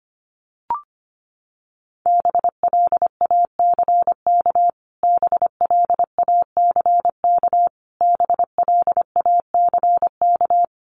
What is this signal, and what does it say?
Morse code, heard as a keyed tone, about 25 wpm, 700 Hz. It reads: BLACK BLACK BLACK
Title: Top 500 Words Repeated 25wpm